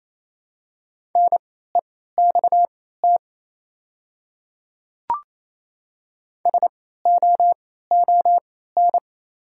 Morse code, heard as a keyed tone, about 28 wpm, spaced as for 16 wpm, 700 Hz. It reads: NEXT SOON
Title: 3x Character Spacing - Top 500 Words - 28wpm